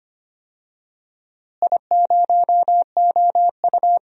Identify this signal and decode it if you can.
Morse code, heard as a keyed tone, about 25 wpm, 700 Hz. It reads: I0OU
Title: DXCC-100 Call Signs - 4 Characters - Repeated 2x - 25wpm